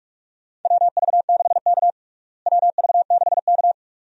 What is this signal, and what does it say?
Morse code, heard as a keyed tone, about 46 wpm, 700 Hz. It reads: WV6K WV6K
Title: USA Call Signs - 4 Characters - Repeated 2x - 45wpm